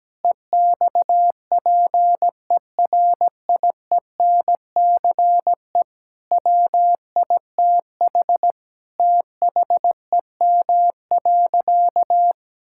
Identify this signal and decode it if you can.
Morse code, heard as a keyed tone, about 17 wpm, 700 Hz. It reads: EXPERIENCE WITH THEM.
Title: Sentences from Top 1000 Words - Review Part 6 - 17wpm